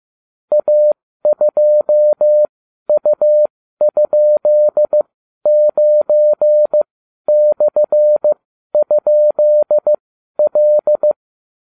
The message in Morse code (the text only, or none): A2U?9/?L